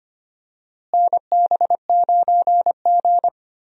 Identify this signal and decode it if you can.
Morse code, heard as a keyed tone, about 25 wpm, 700 Hz. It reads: NB9G
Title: USA Call Signs - 4 Characters - Repeated 3x - 25wpm